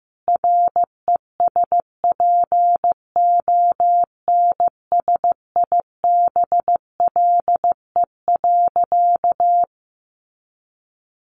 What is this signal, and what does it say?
Morse code, heard as a keyed tone, about 15 wpm, 700 Hz. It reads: RESPONSIBLE.
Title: A Sound Of Thunder 15wpm